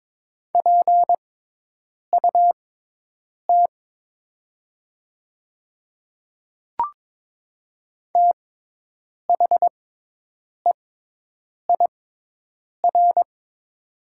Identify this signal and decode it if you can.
Morse code, heard as a keyed tone, about 23 wpm, spaced as for 8 wpm, 700 Hz. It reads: PUT THEIR
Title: 6x Character Spacing - Top 300 Words - 22wpm